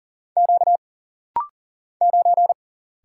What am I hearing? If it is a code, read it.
Morse code, heard as a keyed tone, about 40 wpm, 700 Hz. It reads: Q 9
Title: Single Letter-Number Alt - Rapid-Fire 40wpm